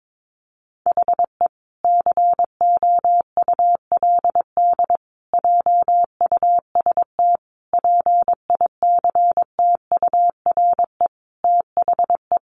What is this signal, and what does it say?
Morse code, heard as a keyed tone, about 22 wpm, 700 Hz. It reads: HE COULD JUST PICTURE THE
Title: Sentences from Top 1100 Words 22wpm